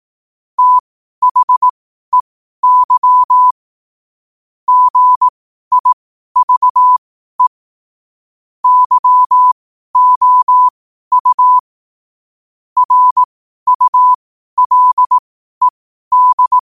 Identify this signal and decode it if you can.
Morse code, heard as a keyed tone, about 18 wpm, spaced as for 12 wpm, 1000 Hz. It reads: THEY GIVE YOU RULED
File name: qotdfiles / 12wpm